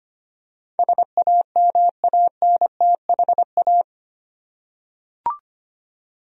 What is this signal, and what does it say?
Morse code, heard as a keyed tone, about 25 wpm, 700 Hz. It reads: SAMANTHA
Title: USA Common Names - 25wpm